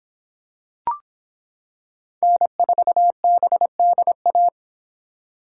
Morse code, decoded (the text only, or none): N4BDA